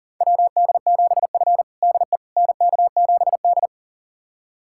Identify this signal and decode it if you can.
Morse code, heard as a keyed tone, about 40 wpm, 700 Hz. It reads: WD7F DE NK7D
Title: USA Callsign DE Callsign - 4 Characters - 40wpm